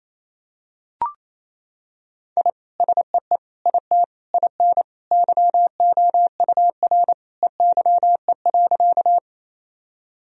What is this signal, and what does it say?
Morse code, heard as a keyed tone, about 28 wpm, 700 Hz. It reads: I SEE IT IN YOUR EYE.
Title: Sentences from Top 1000 Words - Review Part 1 - 28wpm